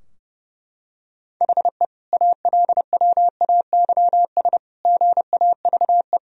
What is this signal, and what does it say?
Morse code, heard as a keyed tone, about 30 wpm, 700 Hz. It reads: HE ALWAYS GAVE
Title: Sentences from Top 1000 Words - Review Part 3 - 30wpm